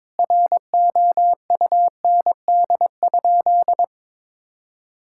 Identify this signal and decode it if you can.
Morse code, heard as a keyed tone, about 21 wpm, 700 Hz. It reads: ROUND?
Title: Sentences from Top 1000 Words - Review Part 6 - 22wpm